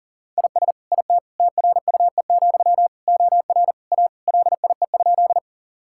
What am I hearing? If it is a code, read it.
Morse code, heard as a keyed tone, about 40 wpm, 700 Hz. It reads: IS IT TRUE, OR A LIE?
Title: Sentences from Top 1600 Words 40wpm